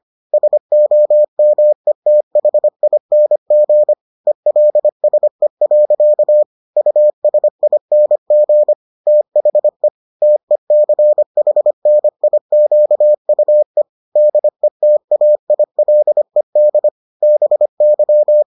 Morse code, heard as a keyed tone, about 25 wpm, 600 Hz. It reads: SOMETHING ELSE. USING THE TECHNIQUE DETAILED BY